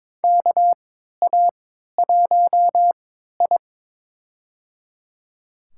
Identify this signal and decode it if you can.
Morse code, heard as a keyed tone, about 22 wpm, spaced as for 13 wpm, 700 Hz. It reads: KA1I
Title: USA Call Signs - 4 Characters - 3x Character Spacing - 22wpm